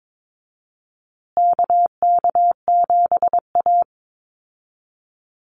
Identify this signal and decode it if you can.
Morse code, heard as a keyed tone, about 22 wpm, 700 Hz. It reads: KK7A
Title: Call Signs Repeated 2x - 22wpm